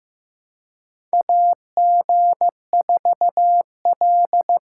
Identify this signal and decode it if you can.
Morse code, heard as a keyed tone, about 15 wpm, 700 Hz. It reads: AG4L
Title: USA Call Signs - 4 Characters - 2x Character Spacing - 15wpm